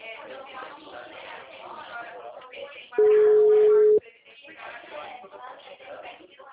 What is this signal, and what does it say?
Ringback tone (Europe, 425 Hz), with speech.
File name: out- -2026- - - .73349